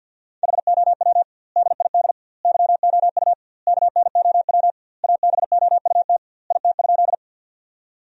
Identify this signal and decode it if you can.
Morse code, heard as a keyed tone, about 50 wpm, 700 Hz. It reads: HOW DID YOU KNOW ABOUT IT?
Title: 2x Word-Spacing - Sentences from Top 100 Words - 50wpm